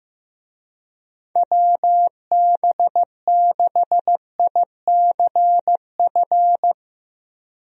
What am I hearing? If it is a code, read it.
Morse code, heard as a keyed tone, about 15 wpm, 700 Hz. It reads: WB6ICF
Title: USA Call Signs - 6 Characters - Repeated 2x - 15wpm